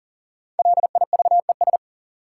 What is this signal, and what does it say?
Morse code, heard as a keyed tone, about 40 wpm, 700 Hz. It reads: LIVES
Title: Sets of 5 Words Encore - New Word Drill - 40wpm